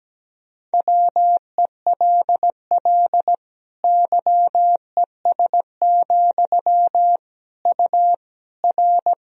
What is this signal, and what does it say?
Morse code, heard as a keyed tone, about 17 wpm, 700 Hz. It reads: WELL YES, U R
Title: Sentences from Top 500 Words and Abbreviations 17wpm